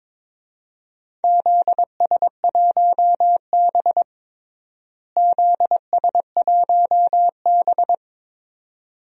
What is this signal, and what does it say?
Morse code, heard as a keyed tone, about 22 wpm, 700 Hz. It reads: ZS1B ZS1B
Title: DXCC-100 Call Signs - 4 Characters - Repeated 2x - 22wpm